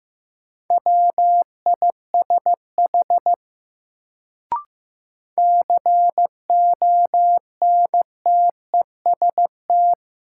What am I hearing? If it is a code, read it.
Morse code, heard as a keyed tone, about 15 wpm, 700 Hz. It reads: WISH CONTEST
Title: Sentences from Top 1600 Words - New Word Drill Alt - 15wpm